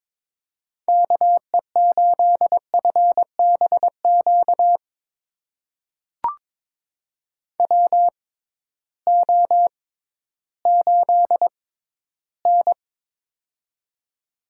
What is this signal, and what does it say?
Morse code, heard as a keyed tone, about 22 wpm, spaced as for 13 wpm, 700 Hz. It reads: KE8FBQ WO8N
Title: USA Call Signs - 6x Character Spacing - 22wpm